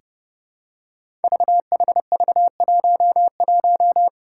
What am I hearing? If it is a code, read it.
Morse code, heard as a keyed tone, about 30 wpm, 700 Hz. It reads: VHV11
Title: Five Challenging Letter-Number Combinations 30wpm